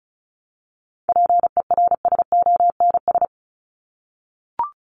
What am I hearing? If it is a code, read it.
Morse code, heard as a keyed tone, about 34 wpm, 700 Hz. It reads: PERSONS
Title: ES:S Suffix - Top 100 Words with Base Words 35wpm